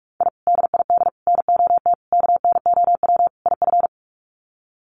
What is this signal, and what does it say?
Morse code, heard as a keyed tone, about 44 wpm, 700 Hz. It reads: I DID NOT KNOW IF
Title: Sets of 5 Words Encore 45wpm